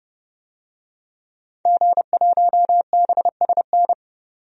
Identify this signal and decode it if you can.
Morse code, heard as a keyed tone, about 30 wpm, 700 Hz. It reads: G1BSN
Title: DXCC-100 Call Signs - 5 Characters - 30wpm